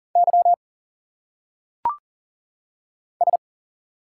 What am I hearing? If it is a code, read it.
Morse code, heard as a keyed tone, about 39 wpm, 700 Hz. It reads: Y S
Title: Single Letters - Phonetic Pronunciation 40wpm